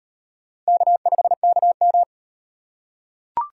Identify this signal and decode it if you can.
Morse code, heard as a keyed tone, about 38 wpm, 700 Hz. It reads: K5KM